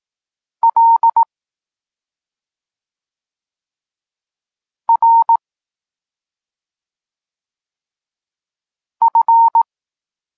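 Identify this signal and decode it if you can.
Morse code, heard as a keyed tone, about 18 wpm, 920 Hz. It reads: L R F
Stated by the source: Polish Morse lesson teaching the letter C, from SP9MOA Niepołomicki Klub Krótkofalowców.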